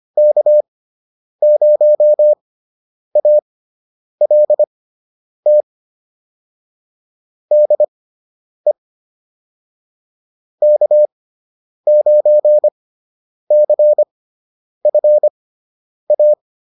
Morse code, heard as a keyed tone, about 25 wpm, spaced as for 9 wpm, 600 Hz. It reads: K0ALT DE K9CFA
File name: QSO 202 09